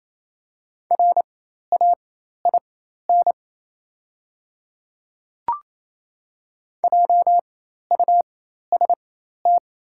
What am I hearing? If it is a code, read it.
Morse code, heard as a keyed tone, about 28 wpm, spaced as for 13 wpm, 700 Hz. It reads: RAIN JUST
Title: 4x Character Spacing - Top 300 Words - 28wpm